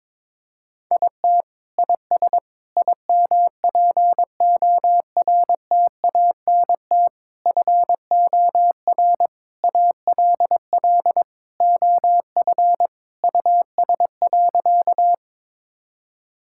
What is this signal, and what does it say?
Morse code, heard as a keyed tone, about 21 wpm, 700 Hz. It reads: IT IS IMPORTANT FOR ALL OF US.
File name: practice / 22wpm / Sentences with 4-Word Phrases - 22wpm